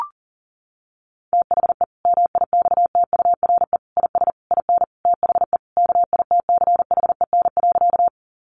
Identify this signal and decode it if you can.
Morse code, heard as a keyed tone, about 40 wpm, 700 Hz. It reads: THE MIXTURE IS IN THE KITCHEN.
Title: Sentences from Top 1500 Words 40wpm